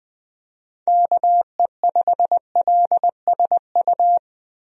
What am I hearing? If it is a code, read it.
Morse code, heard as a keyed tone, about 20 wpm, 700 Hz. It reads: KE5LSU
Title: USA Call Signs - 6 Characters - Repeated 2x - 20wpm